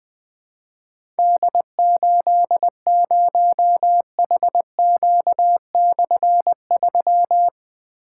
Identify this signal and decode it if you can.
Morse code, heard as a keyed tone, about 20 wpm, 700 Hz. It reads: D80HQ/3
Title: Difficult Call Signs 20wpm